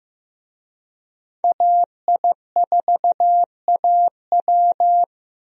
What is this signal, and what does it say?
Morse code, heard as a keyed tone, about 16 wpm, 700 Hz. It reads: AI4AW
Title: USA Call Signs - 5 Characters - Repeated 3x - 15wpm